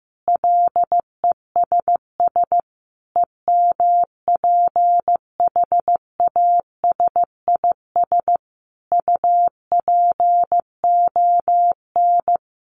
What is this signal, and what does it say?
Morse code, heard as a keyed tone, about 15 wpm, 700 Hz. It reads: LESS EMPHASIS UPON
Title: Sentences from Top 1400 Words 15wpm